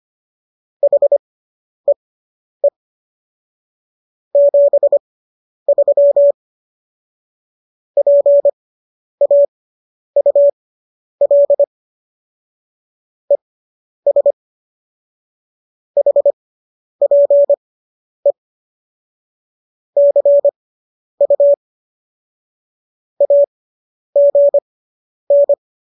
Morse code, heard as a keyed tone, about 25 wpm, spaced as for 10 wpm, 570 Hz. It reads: HEE 73 PAUL ES HPE CU AGN